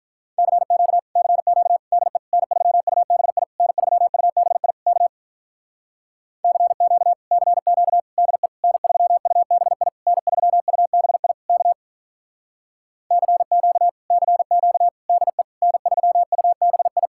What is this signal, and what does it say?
Morse code, heard as a keyed tone, about 44 wpm, 700 Hz. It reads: CQ CQ DE N3UBI N3UBI K CQ CQ DE N3UBI N3UBI K CQ CQ DE N3UBI